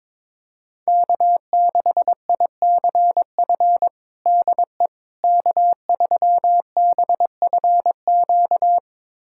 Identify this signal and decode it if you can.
Morse code, heard as a keyed tone, about 22 wpm, 700 Hz. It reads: K6ICF DE K3BFQ
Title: USA Callsign DE Callsign - 5 Characters - 22wpm